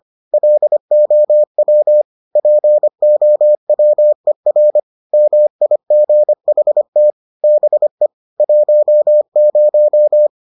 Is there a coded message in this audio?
Yes: Morse code from 0.3 to 10.3 s, heard as a keyed tone, about 25 wpm, 600 Hz. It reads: LOW POWER MIGHT BE 10